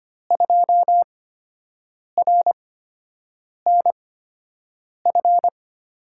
Morse code, heard as a keyed tone, about 25 wpm, spaced as for 7 wpm, 700 Hz. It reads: 2RNF
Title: USA Call Signs - 8x Character Spacing - 25wpm